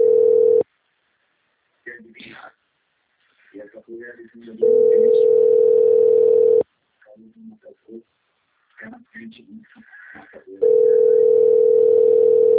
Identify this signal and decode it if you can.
Ringback tone (North America), 440 + 480 Hz, with speech.
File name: out- - - - - .81411